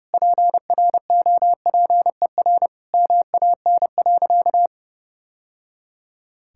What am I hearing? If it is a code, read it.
Morse code, heard as a keyed tone, about 32 wpm, 700 Hz. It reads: PROPER MAN.
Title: Sentences from Top 1000 Words - Review Part 5 - 30wpm